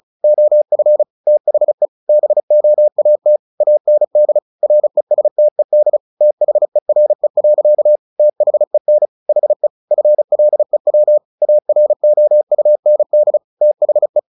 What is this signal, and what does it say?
Morse code, heard as a keyed tone, about 35 wpm, 600 Hz. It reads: OF THE BOAT AND RESTED THERE. THEN HE FLEW AROUND THE
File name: the old man and the sea / the old man and the sea 1 35wpm